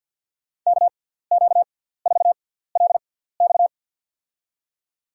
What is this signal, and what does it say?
Morse code, heard as a keyed tone, about 51 wpm, spaced as for 17 wpm, 700 Hz. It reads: KQ4LX
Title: USA Call Signs - 5 Characters - 6x Character Spacing - 50wpm